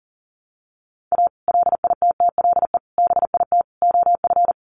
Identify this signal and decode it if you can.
Morse code, heard as a keyed tone, about 40 wpm, 700 Hz. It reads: A LITTLE BIT OF
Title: Sets of 4 Words 40wpm